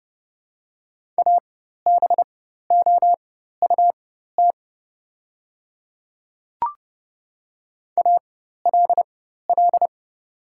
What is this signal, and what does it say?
Morse code, heard as a keyed tone, about 30 wpm, spaced as for 14 wpm, 700 Hz. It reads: ABOUT ALL
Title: 4x Character Spacing - Top 100 Words - 30wpm